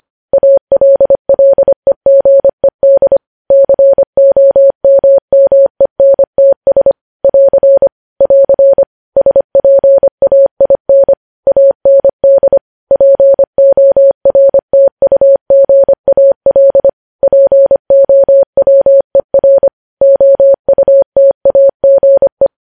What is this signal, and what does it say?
Morse code, heard as a keyed tone, about 25 wpm, 550 Hz. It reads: ALLEGED COMMENTS <AR> <AR> SPAIN AND PORTUGAL POWER OUTAGE
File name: podcast / News headlines-1x-25x25